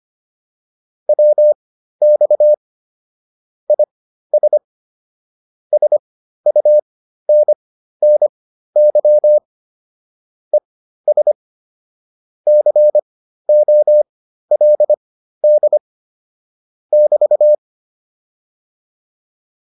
Morse code, heard as a keyed tone, about 25 wpm, spaced as for 13 wpm, 600 Hz. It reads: WX IS SUNNY ES COLD <BT>